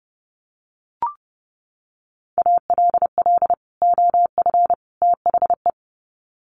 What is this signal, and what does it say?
Morse code, heard as a keyed tone, about 30 wpm, 700 Hz. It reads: ALL OF THE
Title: Sets of 3 Words 30wpm